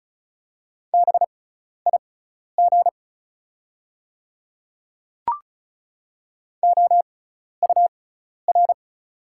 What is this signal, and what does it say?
Morse code, heard as a keyed tone, about 34 wpm, spaced as for 12 wpm, 690 Hz. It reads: BIG OUR
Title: 6x Character Spacing - Top 300 Words - 35wpm